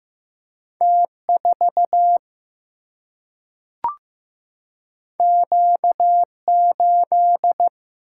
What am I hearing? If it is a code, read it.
Morse code, heard as a keyed tone, about 15 wpm, 700 Hz. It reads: T 4 Q 8
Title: Single-Letter Single-Number 15wpm